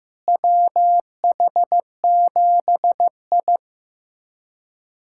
Morse code, heard as a keyed tone, about 15 wpm, 700 Hz. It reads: WH7I
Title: USA Callsign DE Callsign - 4 Characters - 15wpm